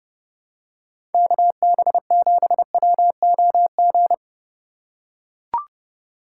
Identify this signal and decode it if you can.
Morse code, heard as a keyed tone, about 29 wpm, 700 Hz. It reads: KB7WOG